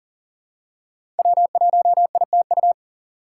Morse code, heard as a keyed tone, about 40 wpm, 700 Hz. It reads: W1ITU
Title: USA Call Signs - 4x Character Spacing - 40wpm